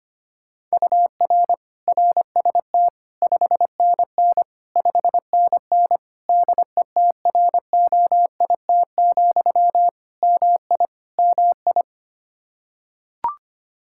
Morse code, heard as a keyed tone, about 25 wpm, 700 Hz. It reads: UR RST 5NN 5NN DETROIT, MI MI
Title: Practice QSO - Exchange 1 - Beginning - 25wpm